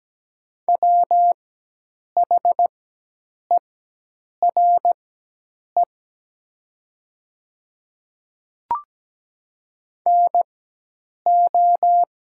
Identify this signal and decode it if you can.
Morse code, heard as a keyed tone, about 17 wpm, spaced as for 8 wpm, 700 Hz. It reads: WHERE NO
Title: 4x Character Spacing - Top 300 Words - 17wpm